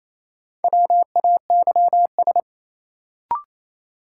Morse code, heard as a keyed tone, about 28 wpm, 700 Hz. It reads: WAYS
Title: Sets of 5 Words Encore - New Word Drill - 28wpm